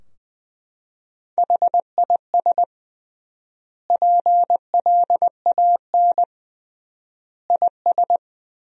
Morse code, heard as a keyed tone, about 20 wpm, 700 Hz. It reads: HIS PLAN IS
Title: 3x Word-Spacing - Sentences from Top 500 Words - 20wpm